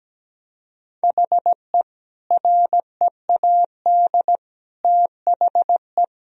Morse code, heard as a keyed tone, about 17 wpm, 700 Hz. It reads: HE READ THE
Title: Sentences from Top 1000 Words - Review Part 1 - 17wpm